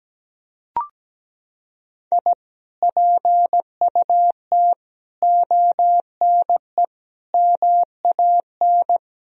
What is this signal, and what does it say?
Morse code, heard as a keyed tone, about 17 wpm, 700 Hz. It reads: I PUT ONE MAN
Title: Sentences from Top 1000 Words - Review Part 2 - 17wpm